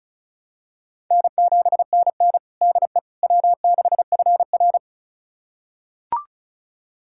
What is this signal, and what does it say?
Morse code, heard as a keyed tone, about 36 wpm, 700 Hz. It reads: N7NN DE W6FR
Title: USA Callsign DE Callsign - 4 Characters - 35wpm